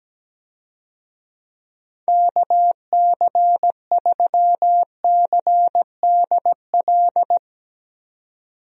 Morse code, heard as a keyed tone, about 17 wpm, 700 Hz. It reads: KC3CDL
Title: USA Call Signs - 6 Characters - 17wpm